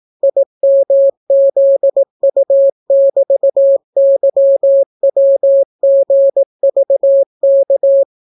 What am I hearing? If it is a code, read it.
Morse code, heard as a keyed tone, about 18 wpm, 550 Hz. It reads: IMZU<BT>YWGVK